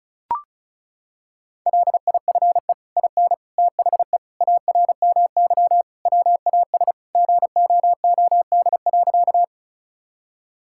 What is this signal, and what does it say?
Morse code, heard as a keyed tone, about 36 wpm, 700 Hz. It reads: LIFE IN THE ARMY WAS GOOD.
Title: Sentences from Top 1000 Words - Review Part 5 - 35wpm